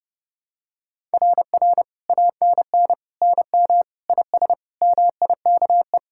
Morse code, heard as a keyed tone, about 30 wpm, 700 Hz. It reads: RR ANN NM IS MIKE
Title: SOTA QSO Practice 30wpm